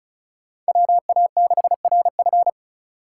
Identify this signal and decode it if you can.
Morse code, heard as a keyed tone, about 36 wpm, 700 Hz. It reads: WA6RF
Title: USA Call Signs - 2x Character Spacing - 35wpm